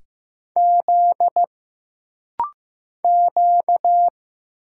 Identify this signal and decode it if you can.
Morse code, heard as a keyed tone, about 15 wpm, 700 Hz. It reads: Z Q
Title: Single Letters - Rapid-Fire 15wpm